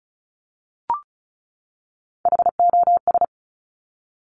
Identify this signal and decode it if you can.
Morse code, heard as a keyed tone, about 36 wpm, 700 Hz. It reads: HOS
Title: Top Three Letter Combinations in Words 35wpm